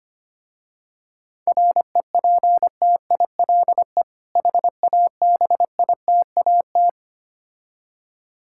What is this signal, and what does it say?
Morse code, heard as a keyed tone, about 25 wpm, 700 Hz. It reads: REPTILE HABITAT
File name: practice / 25wpm / Difficult Words with Hints - 25wpm